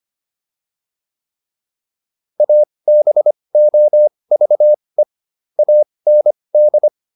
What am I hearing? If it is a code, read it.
Morse code, heard as a keyed tone, about 25 wpm, spaced as for 20 wpm, 600 Hz. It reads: ABOVE AND